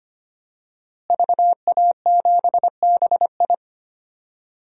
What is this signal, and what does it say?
Morse code, heard as a keyed tone, about 25 wpm, 700 Hz. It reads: VA7BI
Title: DXCC-100 Call Signs - 5 Characters - Repeated 2x - 25wpm